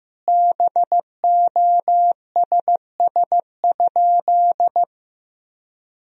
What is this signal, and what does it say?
Morse code, heard as a keyed tone, about 16 wpm, 700 Hz. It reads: BOSS?
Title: Sentences from Top 1000 Words - Review Part 4 - 15wpm